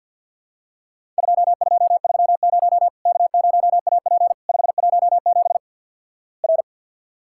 Morse code, heard as a keyed tone, about 50 wpm, 700 Hz. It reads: 2230 K0AP 517 R
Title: Worked All Europe DX Contest - QTC Exchange - 50wpm